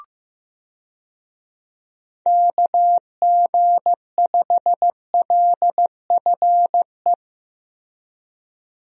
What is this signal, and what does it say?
Morse code, heard as a keyed tone, about 16 wpm, 700 Hz. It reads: KG5LFE
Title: USA Call Signs - 6 Characters - Repeated 3x - 15wpm